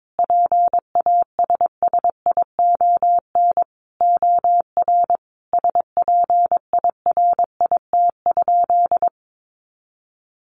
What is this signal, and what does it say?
Morse code, heard as a keyed tone, about 22 wpm, 700 Hz. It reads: PASSION OR SPIRIT?
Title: Sentences from Top 1400 Words 22wpm